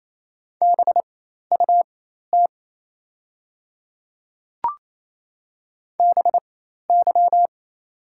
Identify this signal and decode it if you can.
Morse code, heard as a keyed tone, about 27 wpm, spaced as for 13 wpm, 690 Hz. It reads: BUT BY
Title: 4x Character Spacing - Top 200 Words - 28wpm